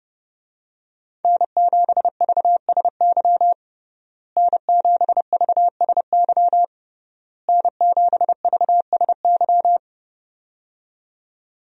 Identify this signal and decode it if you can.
Morse code, heard as a keyed tone, about 30 wpm, 700 Hz. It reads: N7VSY N7VSY N7VSY
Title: USA Call Signs - 5 Characters - Repeated 3x - 30wpm